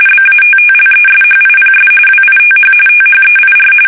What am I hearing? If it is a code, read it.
RTTY, about 45.45 baud. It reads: BATTLESHIPS WITH THEIR
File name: fsk850 / battleshipnj